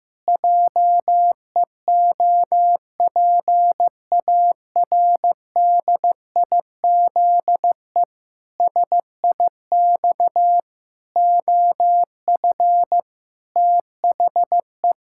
Morse code, heard as a keyed tone, about 16 wpm, 700 Hz. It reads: JEOPARDIZE SIX OF THE
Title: Pangram Challenge v2 - 15wpm